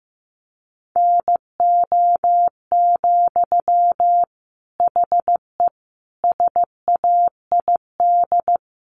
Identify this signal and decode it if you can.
Morse code, heard as a keyed tone, about 15 wpm, 700 Hz. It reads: NO, HE SAID
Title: Sentences from Top 1100 Words 15wpm